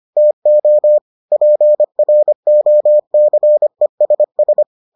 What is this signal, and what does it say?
Morse code, heard as a keyed tone, about 25 wpm, 600 Hz. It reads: TO PROCESS